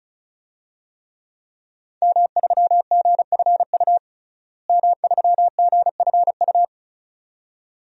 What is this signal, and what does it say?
Morse code, heard as a keyed tone, about 36 wpm, 700 Hz. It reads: M3GFU M3GFU
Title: DXCC-100 Call Signs - 5 Characters - Repeated 2x - 35wpm